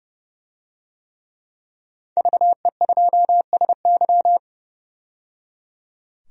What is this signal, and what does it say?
Morse code, heard as a keyed tone, about 30 wpm, 700 Hz. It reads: VE2SY
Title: DXCC-100 Call Signs - 5 Characters - 30wpm